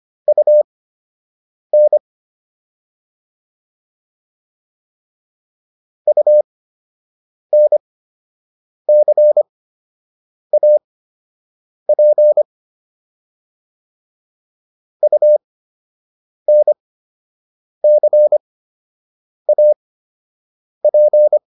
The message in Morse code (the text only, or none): UN UNCAP UNCAP